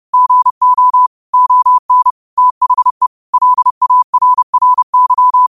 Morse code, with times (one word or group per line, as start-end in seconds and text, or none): 0.1-1.1 GO
1.3-2.1 ON
2.4-3.1 THE
3.3-5.5 LARRY